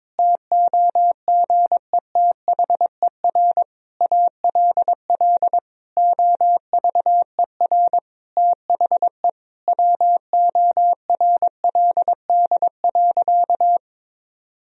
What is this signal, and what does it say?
Morse code, heard as a keyed tone, about 22 wpm, 700 Hz. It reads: TOGETHER ALL OVER THE WORLD.
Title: Sentences from Top 1000 Words - Review Part 2 - 22wpm